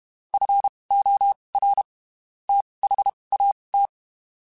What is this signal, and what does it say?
Morse code, heard as a keyed tone, about 32 wpm, spaced as for 23 wpm, 800 Hz. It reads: FOR THAT
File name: (spaced) / QSOs Spaced-mp3s / WA7QOX 1 @30 sp